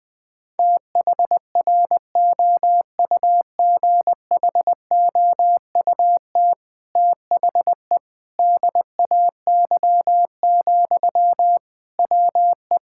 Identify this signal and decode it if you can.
Morse code, heard as a keyed tone, about 21 wpm, 700 Hz. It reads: THROUGHOUT THE DAY, WE
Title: Sentences from Top 1000 Words - Review Part 5 - 20wpm